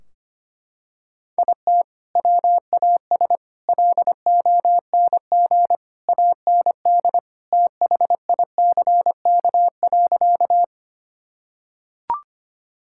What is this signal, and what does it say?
Morse code, heard as a keyed tone, about 25 wpm, 700 Hz. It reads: IT WAS LONG AND THICK.
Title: Sentences from Top 1600 Words 25wpm